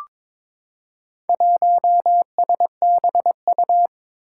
Morse code, heard as a keyed tone, about 22 wpm, 700 Hz. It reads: 1SBU
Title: Four Challenging Letter-Number Combinations 22wpm